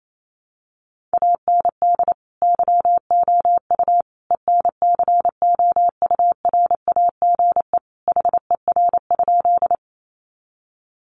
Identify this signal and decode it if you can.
Morse code, heard as a keyed tone, about 28 wpm, 700 Hz. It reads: AND YOU ENCOURAGE HER?
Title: Sentences from Top 1400 Words 28wpm